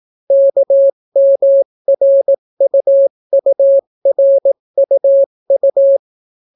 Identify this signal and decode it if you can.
Morse code, heard as a keyed tone, about 18 wpm, 550 Hz. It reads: KMRUURUU